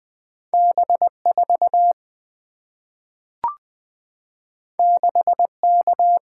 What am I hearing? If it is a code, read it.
Morse code, heard as a keyed tone, about 20 wpm, 700 Hz. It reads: B4 6K
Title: Two Letter-Number Combinations - 20wpm